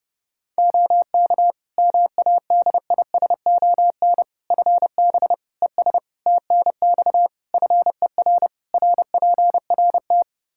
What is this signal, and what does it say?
Morse code, heard as a keyed tone, about 30 wpm, 700 Hz. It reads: OK MADISON FB ES TNX FER RPRT
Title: Practice QSO - Exchange 2 - Info - 30wpm